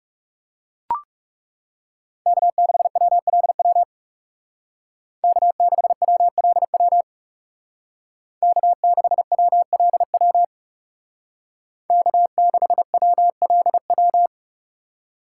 Morse code, heard as a keyed tone, about 35 wpm, 700 Hz. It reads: K6WLW K6WLW K6WLW K6WLW
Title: Speed Racing - USA Call Signs 30wpm